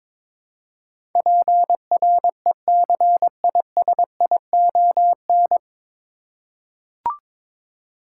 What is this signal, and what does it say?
Morse code, heard as a keyed tone, about 22 wpm, 700 Hz. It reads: PRECISION
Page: Morse Code Ninja practice 22wpm